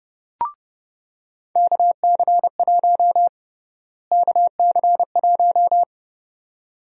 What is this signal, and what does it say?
Morse code, heard as a keyed tone, about 30 wpm, 700 Hz. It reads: KC1 KC1
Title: USA Call Sign Prefixes - Repeated 2x - 30wpm